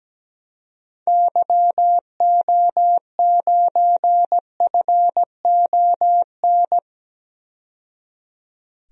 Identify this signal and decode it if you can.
Morse code, heard as a keyed tone, about 17 wpm, 700 Hz. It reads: YO9FON
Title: DXCC-100 Call Signs - 6 Characters - 17wpm